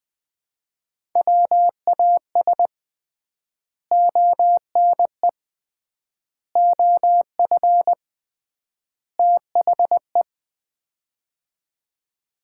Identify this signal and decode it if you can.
Morse code, heard as a keyed tone, about 20 wpm, 700 Hz. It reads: WAS ONE OF THE
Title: 3x Word-Spacing - Sets of 4 Words - 20wpm